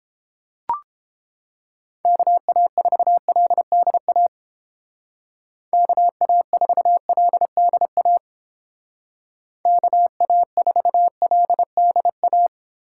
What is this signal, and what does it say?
Morse code, heard as a keyed tone, about 30 wpm, 700 Hz. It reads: KA4LDA KA4LDA KA4LDA